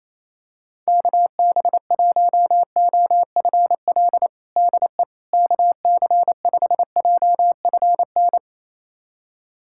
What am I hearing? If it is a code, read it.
Morse code, heard as a keyed tone, about 28 wpm, 700 Hz. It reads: KB1OFL DE KC5JFN
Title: USA Callsign DE Callsign - 6 Characters - 28wpm